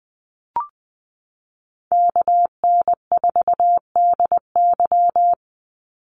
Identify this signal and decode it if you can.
Morse code, heard as a keyed tone, about 20 wpm, 700 Hz. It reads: KN4DY
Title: USA Call Signs 20wpm